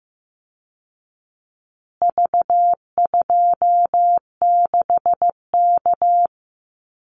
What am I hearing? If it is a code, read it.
Morse code, heard as a keyed tone, about 15 wpm, 700 Hz. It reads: V26K
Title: Call Signs Repeated 2x - 15wpm